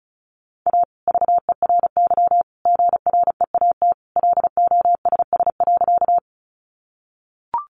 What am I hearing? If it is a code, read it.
Morse code, heard as a keyed tone, about 34 wpm, 700 Hz. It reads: A VERY GREAT LOSS.
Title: Sentences from Top 1400 Words 35wpm